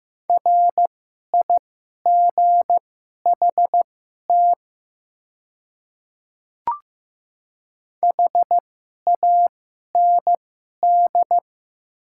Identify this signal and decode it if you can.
Morse code, heard as a keyed tone, about 15 wpm, spaced as for 11 wpm, 700 Hz. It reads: RIGHT HAND
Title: 2x Character Spacing - Top 300 Words - 15wpm